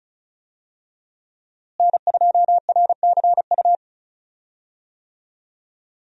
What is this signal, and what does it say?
Morse code, heard as a keyed tone, about 36 wpm, 700 Hz. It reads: N2RCU
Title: USA Call Signs - 5 Characters - Repeated 2x - 35wpm